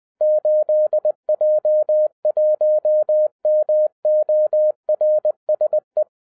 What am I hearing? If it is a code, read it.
Morse code, heard as a keyed tone, about 20 wpm, 610 Hz. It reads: 8J1MORSE